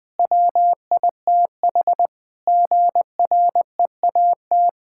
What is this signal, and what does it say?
Morse code, heard as a keyed tone, about 20 wpm, 700 Hz. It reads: WITH GREAT
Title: Sentences from Top 1600 Words 20wpm